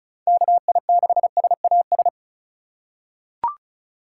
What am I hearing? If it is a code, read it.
Morse code, heard as a keyed tone, about 34 wpm, 690 Hz. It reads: KI6SAS
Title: USA Call Signs - 6 Characters - 2x Character Spacing - 35wpm